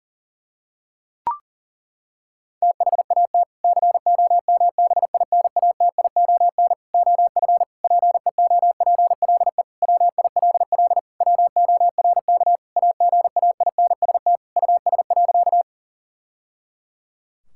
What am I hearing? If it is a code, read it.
Morse code, heard as a keyed tone, about 40 wpm, 700 Hz. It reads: THAT COMBINATION OF PEOPLE WILL WORK AGAINST US.
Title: Sentences from Top 1000 Words - Review Part 6 - 40wpm